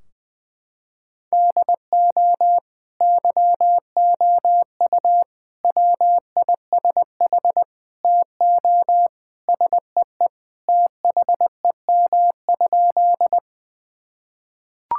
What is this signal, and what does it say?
Morse code, heard as a keyed tone, about 20 wpm, 700 Hz. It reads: DO YOU WISH TO SEE THEM?
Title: Sentences from Top 1600 Words 20wpm